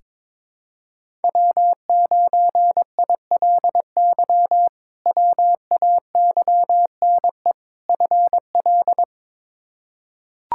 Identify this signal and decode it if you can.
Morse code, heard as a keyed tone, about 22 wpm, 700 Hz. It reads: W9ILY WAYNE FL
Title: K1USN SST QSO Practice - 22wpm